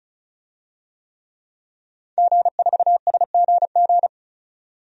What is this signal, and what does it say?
Morse code, heard as a keyed tone, about 36 wpm, 700 Hz. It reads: G4SGG
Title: DXCC-100 Call Signs - 5 Characters - 35wpm